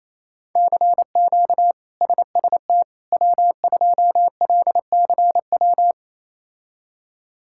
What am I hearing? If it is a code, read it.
Morse code, heard as a keyed tone, about 27 wpm, 700 Hz. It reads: CQ SST W2LCW
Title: K1USN SST QSO Practice - 28wpm